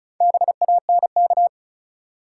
Morse code, heard as a keyed tone, about 34 wpm, 700 Hz. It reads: BANK
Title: Sets of 5 Words - New Word Drill - 35wpm